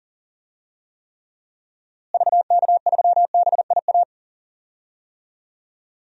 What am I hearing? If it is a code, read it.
Morse code, heard as a keyed tone, about 40 wpm, 700 Hz. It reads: VK3BIA
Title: DXCC-100 Call Signs - 6 Characters - Repeated 2x - 40wpm